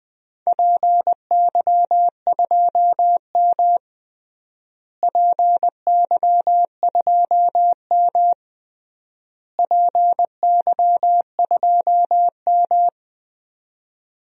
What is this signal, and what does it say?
Morse code, heard as a keyed tone, about 20 wpm, 700 Hz. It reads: PY2M PY2M PY2M
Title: DXCC-100 Call Signs - 4 Characters - Repeated 3x - 20wpm